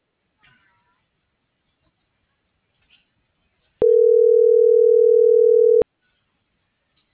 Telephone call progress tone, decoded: ringback tone

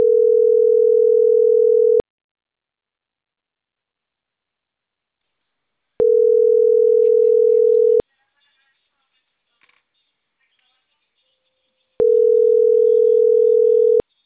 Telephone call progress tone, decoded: ringback tone

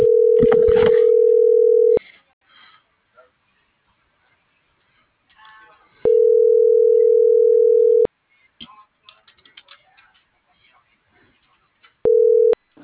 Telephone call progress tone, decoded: ringback tone

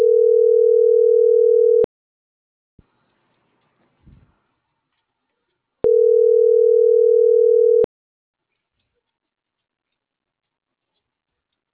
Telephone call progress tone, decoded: ringback tone

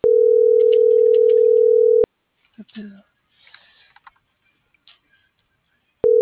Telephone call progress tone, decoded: ringback tone

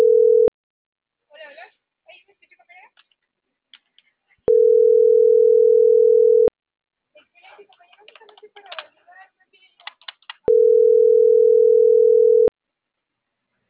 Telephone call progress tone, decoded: ringback tone

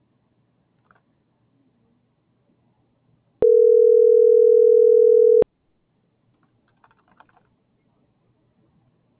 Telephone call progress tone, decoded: ringback tone

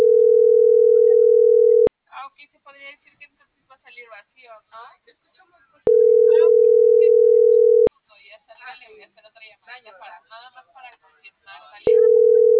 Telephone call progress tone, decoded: ringback tone

